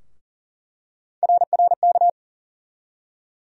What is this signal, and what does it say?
Morse code, heard as a keyed tone, about 40 wpm, 700 Hz. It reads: RRK
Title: USA Call Sign Postfixes - 3 Letters 40wpm